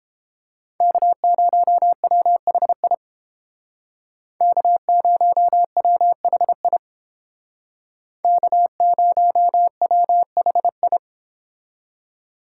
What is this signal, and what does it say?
Morse code, heard as a keyed tone, about 31 wpm, 700 Hz. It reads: K0WHI K0WHI K0WHI